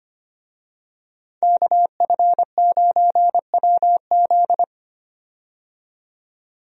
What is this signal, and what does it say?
Morse code, heard as a keyed tone, about 25 wpm, 700 Hz. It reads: KF9WZ
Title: USA Call Signs - 5 Characters - 25wpm